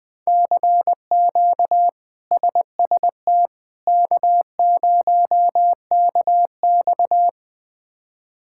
Morse code, heard as a keyed tone, about 20 wpm, 700 Hz. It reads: CQ SST K0KX
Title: K1USN SST QSO Practice - 20wpm